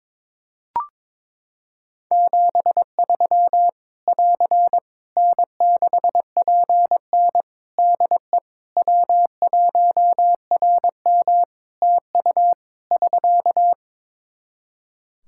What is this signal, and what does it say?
Morse code, heard as a keyed tone, about 22 wpm, 700 Hz. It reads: 73 <AR> N6PN DE W1RM TU <SK>
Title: Practice QSO - Exchange 4 - Ending - 22wpm